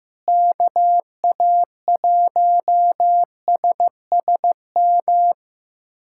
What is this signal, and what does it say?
Morse code, heard as a keyed tone, about 15 wpm, 700 Hz. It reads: KA1SSM
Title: USA Callsign DE Callsign - 6 Characters - 15wpm